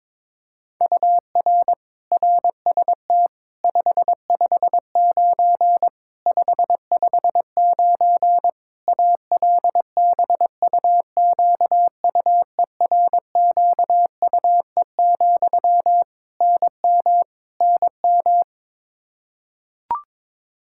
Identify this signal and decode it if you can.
Morse code, heard as a keyed tone, about 22 wpm, 700 Hz. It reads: UR RST 559 559 ALBUQUERQUE, NM NM
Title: Practice QSO - Exchange 1 - Beginning - 22wpm